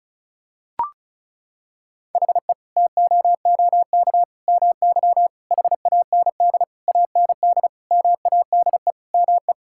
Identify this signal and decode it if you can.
Morse code, heard as a keyed tone, about 36 wpm, 700 Hz. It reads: HE TOOK MY HAND AND MADE ME
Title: Sentences from Top 1000 Words - Review Part 6 - 35wpm